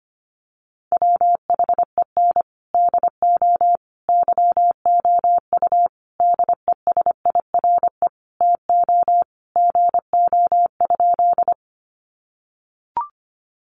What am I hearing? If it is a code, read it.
Morse code, heard as a keyed tone, about 25 wpm, 700 Hz. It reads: WHEN DO YOU DESIRE TO GO?
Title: Sentences from Top 1100 Words 25wpm